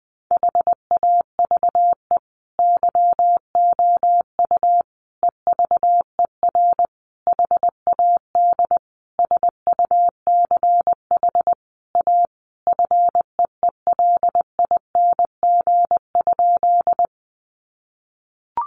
Morse code, heard as a keyed tone, about 20 wpm, 700 Hz. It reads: HAVE YOU EVER HAD SUCH A FEELING?
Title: Sentences from Top 1200 Words 20wpm